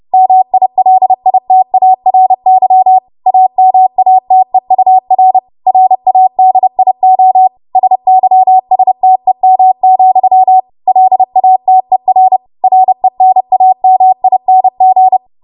Morse code, heard as a keyed tone, about 30 wpm, 750 Hz. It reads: MILITARY AMATEUR RADIO SYSTEM, LATER RENAMING
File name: Archive / 30 WPM / 30WPM